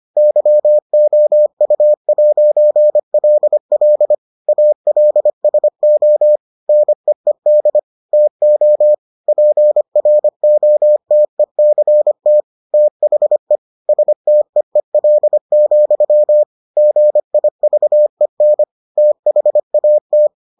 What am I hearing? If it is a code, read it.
Morse code, heard as a keyed tone, about 25 wpm, 600 Hz. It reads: YOU'LL ALSO NEED TO PROTECT THE STEEL, GIVEN THAT